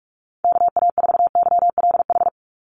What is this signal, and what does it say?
Morse code, heard as a keyed tone, about 44 wpm, 700 Hz. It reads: KA4YLH
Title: USA Call Signs 45wpm